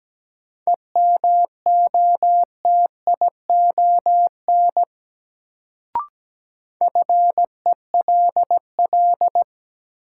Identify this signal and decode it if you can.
Morse code, heard as a keyed tone, about 17 wpm, 700 Hz. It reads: EMOTION FELL
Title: Sentences from Top 1600 Words - New Word Drill Alt - 17wpm